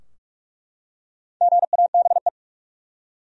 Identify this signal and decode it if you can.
Morse code, heard as a keyed tone, about 43 wpm, 700 Hz. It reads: GABE
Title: USA Common Names - 45wpm